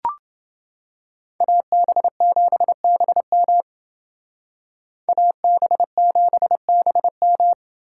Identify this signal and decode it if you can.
Morse code, heard as a keyed tone, about 29 wpm, 700 Hz. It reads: AB7BM AB7BM